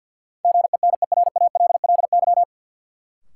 Morse code, heard as a keyed tone, about 53 wpm, 700 Hz. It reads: GENERALLY
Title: Sentences from Top 600 Words - New Word Drill - 50wpm